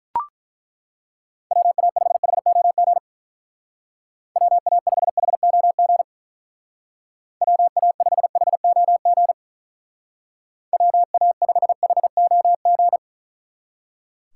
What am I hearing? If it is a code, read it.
Morse code, heard as a keyed tone, about 42 wpm, 700 Hz. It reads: WA5HOG WA5HOG WA5HOG WA5HOG
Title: Speed Racing - USA 2x3 Call Signs 35wpm